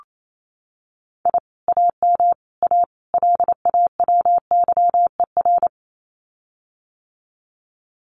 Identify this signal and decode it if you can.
Morse code, heard as a keyed tone, about 28 wpm, 700 Hz. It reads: I AM A LAWYER
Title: Common QSO Phrases Encore 28wpm